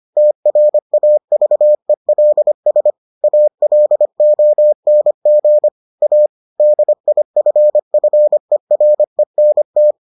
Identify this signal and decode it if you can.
Morse code, heard as a keyed tone, about 25 wpm, 600 Hz. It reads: TRAVELS ALONG A DIFFERENT